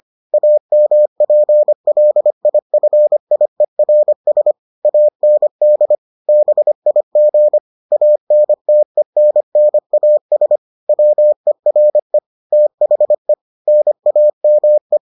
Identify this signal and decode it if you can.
Morse code, heard as a keyed tone, about 25 wpm, 600 Hz. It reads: AMPLIFIERS AND BIG ANTENNAS WERE THE NAME